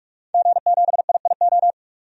Morse code, heard as a keyed tone, about 44 wpm, 690 Hz. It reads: G7IIO
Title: DXCC-100 Call Signs - 5 Characters - 45wpm